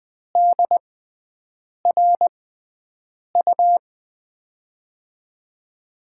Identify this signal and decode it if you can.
Morse code, heard as a keyed tone, about 20 wpm, spaced as for 7 wpm, 700 Hz. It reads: DRU
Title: USA Call Signs - 6x Character Spacing - 20wpm